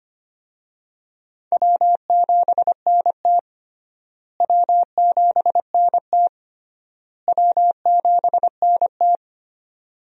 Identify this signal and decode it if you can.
Morse code, heard as a keyed tone, about 25 wpm, 700 Hz. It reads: W7NT W7NT W7NT
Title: USA Call Signs - 4 Characters - Repeated 3x - 25wpm